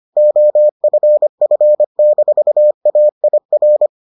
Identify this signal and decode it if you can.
Morse code, heard as a keyed tone, about 25 wpm, 600 Hz. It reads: OFF-AIR